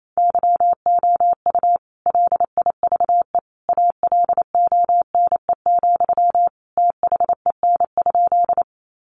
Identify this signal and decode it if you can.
Morse code, heard as a keyed tone, about 28 wpm, 700 Hz. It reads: YOU LIVE ALONE, THEN?
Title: Sentences from Top 1000 Words 28wpm